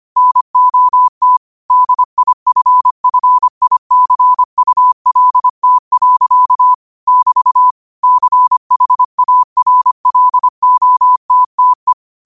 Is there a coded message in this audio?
Yes: Morse code from 0.2 to 11.9 s, heard as a keyed tone, about 25 wpm, 1000 Hz. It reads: NOT DIFFICULT. <BT> CHARLOTTE